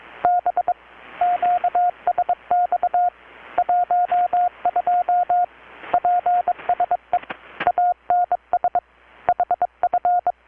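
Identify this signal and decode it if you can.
Morse code, heard as a keyed tone, about 22 wpm, 720 Hz, with radio noise. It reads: B QSX 12 PSE ANS HF
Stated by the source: Morse from KPH's Night of Nights XX maritime radiotelegraphy event, received in Chiapas, Mexico about 3,800 km away on a Kenwood R-600 with a grounded 10 m long-wire antenna.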